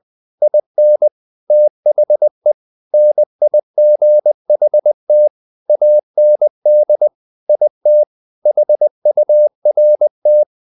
Morse code, heard as a keyed tone, about 20 wpm, 600 Hz. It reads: IN THE NIGHT AND IT HURT
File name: the old man and the sea / the old man and the sea part 1 20wpm